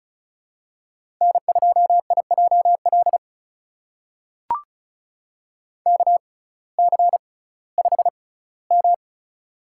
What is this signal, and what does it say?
Morse code, heard as a keyed tone, about 36 wpm, spaced as for 19 wpm, 700 Hz. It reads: N2IJL KC5M
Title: USA Call Signs - 6x Character Spacing - 35wpm